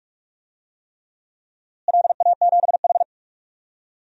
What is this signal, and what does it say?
Morse code, heard as a keyed tone, about 44 wpm, 700 Hz. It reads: LA7H
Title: DXCC-100 Call Signs - 4 Characters - Repeated 3x - 45wpm